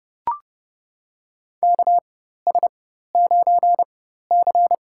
Morse code, heard as a keyed tone, about 30 wpm, spaced as for 14 wpm, 700 Hz. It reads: KS9C
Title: USA Call Signs - 4 Characters - 4x Character Spacing - 30wpm